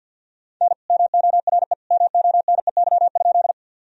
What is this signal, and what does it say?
Morse code, heard as a keyed tone, about 51 wpm, 700 Hz. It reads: N MORE MONEY?
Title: Sentences from Top 400 Words and Abbreviations 50wpm